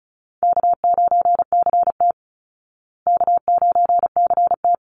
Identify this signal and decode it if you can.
Morse code, heard as a keyed tone, about 34 wpm, 700 Hz. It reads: K9CT K9CT
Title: Easy Call Signs Repeated 2x - 35wpm